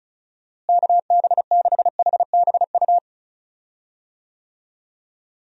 Morse code, heard as a keyed tone, about 36 wpm, 700 Hz. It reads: KB6HBU
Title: USA Call Signs - 6 Characters - Repeated 2x - 35wpm